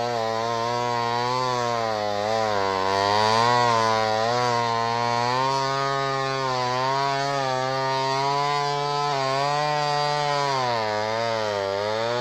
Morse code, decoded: none